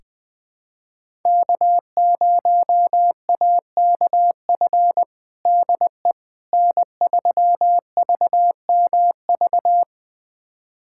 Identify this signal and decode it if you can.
Morse code, heard as a keyed tone, about 20 wpm, 700 Hz. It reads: K0AKF DE N3VMV
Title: USA Callsign DE Callsign - 5 Characters - 20wpm